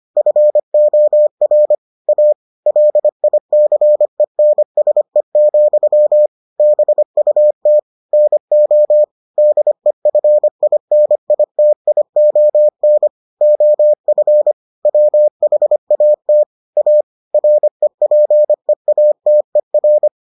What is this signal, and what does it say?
Morse code, heard as a keyed tone, about 25 wpm, 600 Hz. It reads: FOR A LICENSE, BUT NO DEFINITION OF WHAT A REPEATER